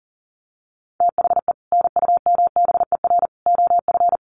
Morse code, heard as a keyed tone, about 40 wpm, 700 Hz. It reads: THE NUMBER OF
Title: Sets of 3 Words 40wpm